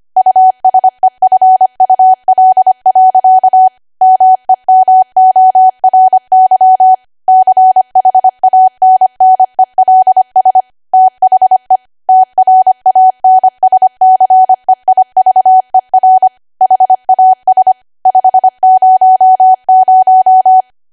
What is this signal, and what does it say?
Morse code, heard as a keyed tone, about 25 wpm, 750 Hz. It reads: USEFUL. MEMORY CHANNELS THE TRANSCEIVER HAS 500
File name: Archive / 25 WPM / 25WPM